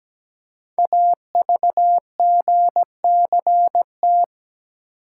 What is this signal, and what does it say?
Morse code, heard as a keyed tone, about 17 wpm, 700 Hz. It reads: AVGCT
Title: Code Groups - 5 Letter:Numbers - 17wpm